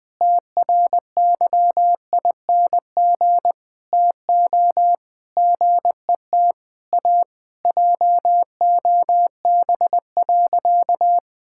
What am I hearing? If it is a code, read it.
Morse code, heard as a keyed tone, about 20 wpm, 700 Hz. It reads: TRYING TO GET A JOB.
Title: Sentences with 3-Word Phrases - 20wpm